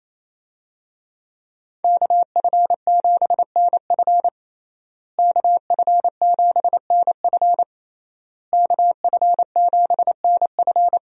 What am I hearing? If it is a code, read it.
Morse code, heard as a keyed tone, about 28 wpm, 700 Hz. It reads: KF7NF KF7NF KF7NF
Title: USA Call Signs - 5 Characters - Repeated 3x - 28wpm